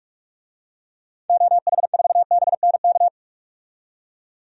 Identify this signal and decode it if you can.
Morse code, heard as a keyed tone, about 46 wpm, 700 Hz. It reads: OH4BNK
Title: DXCC-100 Call Signs - 6 Characters - 45wpm